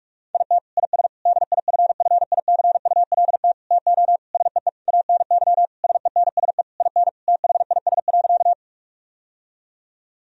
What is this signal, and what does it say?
Morse code, heard as a keyed tone, about 46 wpm, 700 Hz. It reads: IT IS DIFFICULT TO SEE ANY SENSE IN THIS.
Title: Sentences from Top 1000 Words - Review Part 4 - 45wpm